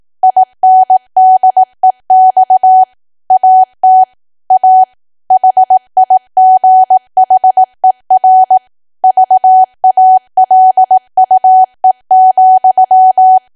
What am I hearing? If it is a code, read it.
Morse code, heard as a keyed tone, about 18 wpm, 750 Hz. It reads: INDEX AT A HIGHER VALUE,
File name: cwa / W1AW-Level2 / 18WPM